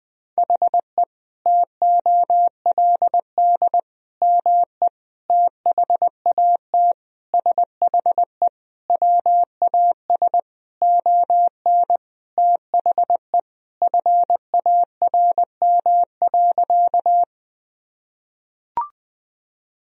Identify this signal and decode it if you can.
Morse code, heard as a keyed tone, about 21 wpm, 700 Hz. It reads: HE TOLD ME THAT SHE WAS ON THE FARM.
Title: Sentences from Top 1000 Words - Review Part 2 - 20wpm